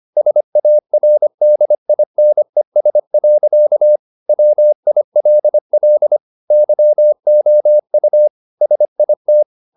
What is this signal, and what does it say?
Morse code, heard as a keyed tone, about 25 wpm, 600 Hz. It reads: SARDINES. WILL YOU SIT
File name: the old man and the sea part 1 25wpm